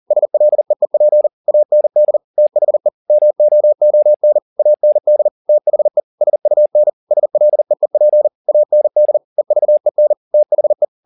Morse code, heard as a keyed tone, about 40 wpm, 600 Hz. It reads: SLEEP AND THE MOON AND THE SUN SLEEP AND EVEN THE
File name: the old man and the sea 2 40wpm